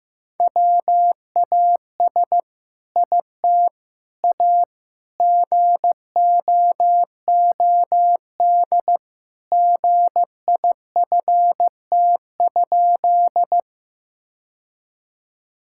Morse code, heard as a keyed tone, about 16 wpm, 700 Hz. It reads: WAS IT A GOOD GIFT?
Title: Sentences from Top 1000 Words - Review Part 6 - 15wpm